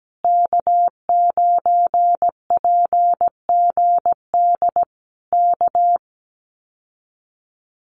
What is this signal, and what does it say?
Morse code, heard as a keyed tone, about 17 wpm, 700 Hz. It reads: K9PGD K
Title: Common QSO Phrases Encore 17wpm